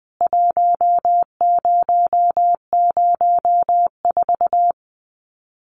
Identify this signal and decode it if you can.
Morse code, heard as a keyed tone, about 20 wpm, 700 Hz. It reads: 1004
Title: CWT Exchange 20wpm